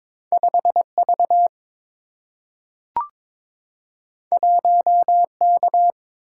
Morse code, heard as a keyed tone, about 22 wpm, 700 Hz. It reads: 5V 1K
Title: Single-Letter Single-Number 22wpm